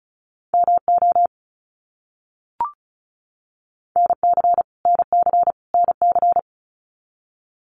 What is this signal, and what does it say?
Morse code, heard as a keyed tone, about 34 wpm, 700 Hz. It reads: MO NC NC NC
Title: US States Repeated 3x 35wpm